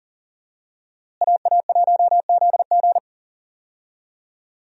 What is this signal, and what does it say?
Morse code, heard as a keyed tone, about 37 wpm, 700 Hz. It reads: AA1ZG
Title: USA Call Signs - 5 Characters - Repeated 3x - 40wpm